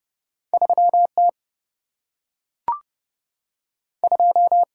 Morse code, heard as a keyed tone, about 30 wpm, 700 Hz. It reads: 3 T 2
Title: Single-Letter Single-Number 30wpm